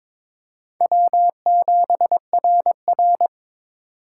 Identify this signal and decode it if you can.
Morse code, heard as a keyed tone, about 22 wpm, 700 Hz. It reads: W7RR
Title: USA Call Signs - 4 Characters - 6x Character Spacing - 22wpm